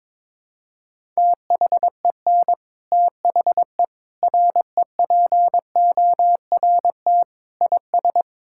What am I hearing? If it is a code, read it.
Morse code, heard as a keyed tone, about 21 wpm, 700 Hz. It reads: THEN THE REPORT IS
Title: Sentences from Top 1000 Words - Review Part 6 - 22wpm